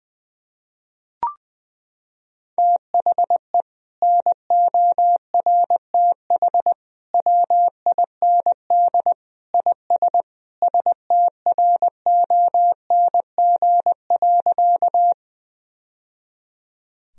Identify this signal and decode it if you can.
Morse code, heard as a keyed tone, about 21 wpm, 700 Hz. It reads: THE NORTH WIND IS STRONG.
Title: Sentences from Top 1000 Words - Review Part 2 - 20wpm